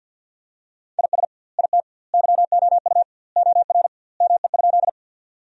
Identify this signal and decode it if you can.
Morse code, heard as a keyed tone, about 50 wpm, 700 Hz. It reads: IS IT YOU OR ME?
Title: 2x Word-Spacing - Sentences from Top 200 Words - 50wpm